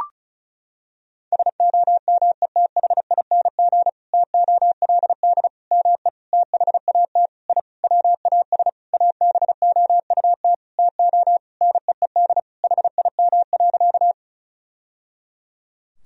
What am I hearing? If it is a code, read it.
Morse code, heard as a keyed tone, about 36 wpm, 700 Hz. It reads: SOMETHING TOLD ME THAT I WAS ABOUT TO NEED HIM.
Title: Sentences from Top 1000 Words - Review Part 1 - 35wpm